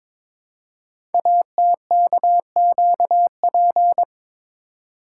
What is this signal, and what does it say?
Morse code, heard as a keyed tone, about 22 wpm, 700 Hz. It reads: ATKQP
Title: Code Groups - 5 Letters - 22wpm